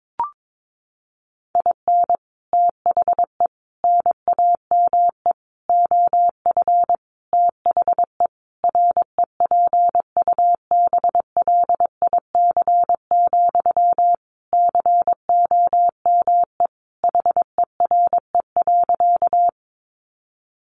Morse code, heard as a keyed tone, about 22 wpm, 700 Hz. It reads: IN THE NAME OF THE REPUBLIC, COME HERE.
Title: Sentences from Top 1500 Words 22wpm